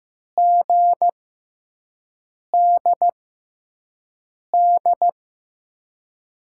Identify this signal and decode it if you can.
Morse code, heard as a keyed tone, about 16 wpm, spaced as for 5 wpm, 700 Hz. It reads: GDD
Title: USA Call Signs - 5 Characters - 6x Character Spacing - 15wpm